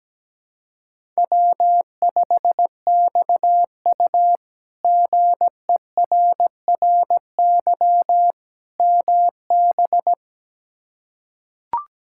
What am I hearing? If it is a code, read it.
Morse code, heard as a keyed tone, about 17 wpm, 700 Hz. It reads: W5XU GERRY MB
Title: K1USN SST QSO Practice - 17wpm